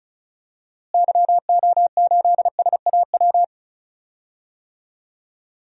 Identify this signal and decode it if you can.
Morse code, heard as a keyed tone, about 36 wpm, 700 Hz. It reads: YO8SAW
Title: DXCC-100 Call Signs - 6 Characters - Repeated 3x - 35wpm